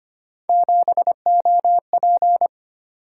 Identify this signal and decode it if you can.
Morse code, heard as a keyed tone, about 25 wpm, 700 Hz. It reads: 7OP
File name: Three Letter-Number Combinations - 25wpm